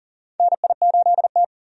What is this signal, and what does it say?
Morse code, heard as a keyed tone, about 37 wpm, 700 Hz. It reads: NI8T
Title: USA Call Signs - 4 Characters - 40wpm